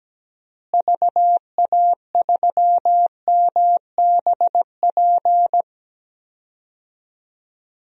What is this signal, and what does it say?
Morse code, heard as a keyed tone, about 17 wpm, 700 Hz. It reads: VA3MBP
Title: DXCC-100 Call Signs - 6 Characters - Repeated 2x - 17wpm